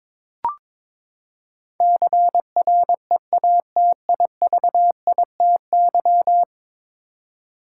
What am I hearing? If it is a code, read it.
Morse code, heard as a keyed tone, about 22 wpm, 700 Hz. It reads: CREATIVITY
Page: morse-code-ninja-practice-22wpm